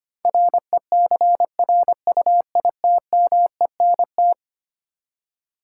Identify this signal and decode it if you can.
Morse code, heard as a keyed tone, about 25 wpm, 700 Hz. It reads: RECRUITMENT
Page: morse-code-ninja-practice-25wpm